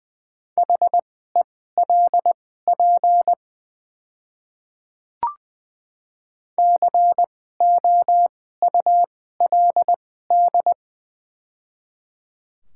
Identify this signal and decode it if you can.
Morse code, heard as a keyed tone, about 20 wpm, spaced as for 14 wpm, 700 Hz. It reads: HELP COULD
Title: 2x Character Spacing - Top 200 Words - 20wpm